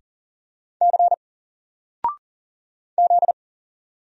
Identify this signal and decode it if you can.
Morse code, heard as a keyed tone, about 40 wpm, 700 Hz. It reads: C Z
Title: Single Character (Letter:Number:Punctuation:Prosign) Alt 40wpm